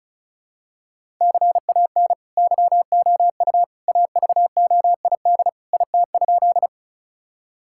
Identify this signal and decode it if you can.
Morse code, heard as a keyed tone, about 36 wpm, 700 Hz. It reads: CAN YOU AVOID IT?
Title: Sentences from Top 1000 Words - Review Part 6 - 35wpm